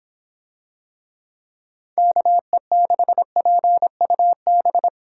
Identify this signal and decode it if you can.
Morse code, heard as a keyed tone, about 26 wpm, 700 Hz. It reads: KE6PUB